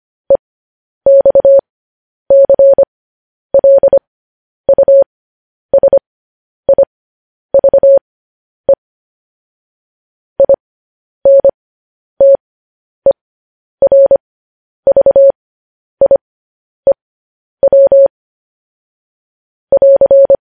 Morse code, heard as a keyed tone, about 25 wpm, spaced as for 10 wpm, 550 Hz. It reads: EXCLUSIVE INTERVIEW <AR>
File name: podcast / News headlines-1x-25x10